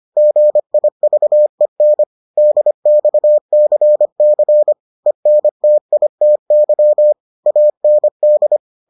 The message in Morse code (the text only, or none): GIVEN DXCC ENTITY AND